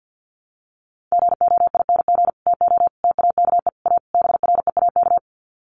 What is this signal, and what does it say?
Morse code, heard as a keyed tone, about 49 wpm, 700 Hz. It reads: GOING TO TAKE A BREAK
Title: Sets of 5 Words 50wpm